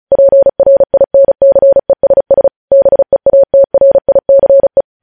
Morse code, heard as a keyed tone, about 36 wpm, 550 Hz. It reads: PRINCESS BEATRICE
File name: podcast / News headlines-1x-35x35